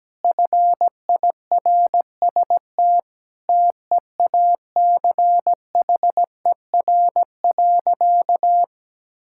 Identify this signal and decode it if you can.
Morse code, heard as a keyed tone, about 17 wpm, 700 Hz. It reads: FIRST TEACHER.
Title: Sentences from Top 1000 Words - Review Part 6 - 17wpm